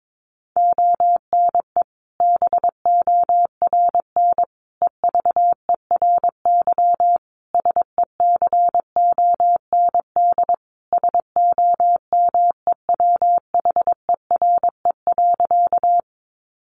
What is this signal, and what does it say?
Morse code, heard as a keyed tone, about 22 wpm, 700 Hz. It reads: ONE BORN EVERY SECOND SOMEWHERE.
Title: Sentences from Top 1400 Words 22wpm